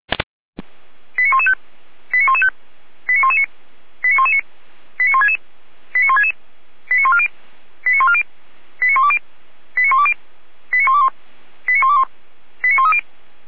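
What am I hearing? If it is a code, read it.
Five-tone selective call (ZVEI), heard as quick runs of five tones, tones of about 70 ms. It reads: 89115 89115 89119 89119 89166 89166 89140 89140 89120 89120 89121 89121 89130
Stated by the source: SDR receiver recording of ZVEI five-tone selective call 89099, from feuerwehr-orlamuende.org (fire brigade Orlamünde).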